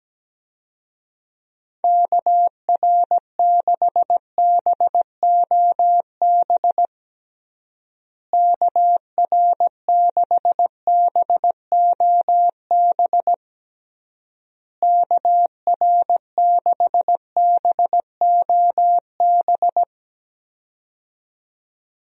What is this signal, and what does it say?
Morse code, heard as a keyed tone, about 17 wpm, 700 Hz. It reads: KR6BOB KR6BOB KR6BOB
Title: USA Call Signs - 6 Characters - Repeated 3x - 17wpm